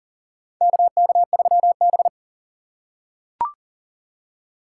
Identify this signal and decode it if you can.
Morse code, heard as a keyed tone, about 40 wpm, 700 Hz. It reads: KK3B
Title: USA Call Signs - 4 Characters - 3x Character Spacing - 40wpm